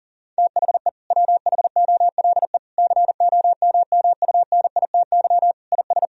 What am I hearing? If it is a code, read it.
Morse code, heard as a keyed tone, about 40 wpm, 700 Hz. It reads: THE WHOLE COMMUNITY IS
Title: Sentences from Top 1000 Words - Review Part 4 - 40wpm